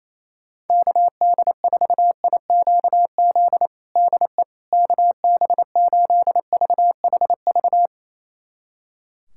Morse code, heard as a keyed tone, about 28 wpm, 700 Hz. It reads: KD4IQZ DE KB8VHV
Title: USA Callsign DE Callsign - 6 Characters - 28wpm